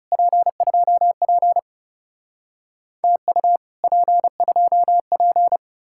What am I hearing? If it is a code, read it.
Morse code, heard as a keyed tone, about 35 wpm, 700 Hz. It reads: P2P TU P2P
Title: Speed-Racing Common QSO Phrases - 30wpm